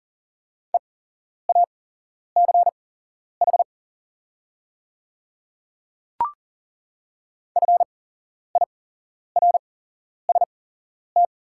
Morse code, heard as a keyed tone, about 38 wpm, spaced as for 11 wpm, 700 Hz. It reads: EACH FIRST
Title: 8x Character Spacing - Top 100 Words - 40wpm